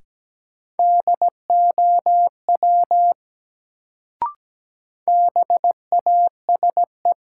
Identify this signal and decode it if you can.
Morse code, heard as a keyed tone, about 17 wpm, 700 Hz. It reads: DOW BASE
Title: Sets of 5 Words - New Word Drill - 17wpm